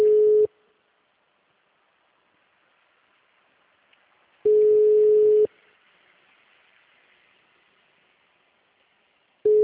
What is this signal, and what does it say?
Ringback tone (Europe, 425 Hz).
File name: out- -2005- - - .35863